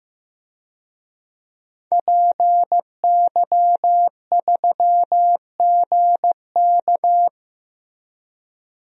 Morse code, heard as a keyed tone, about 15 wpm, 700 Hz. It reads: PY3GK
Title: DXCC-100 Call Signs - 5 Characters - 15wpm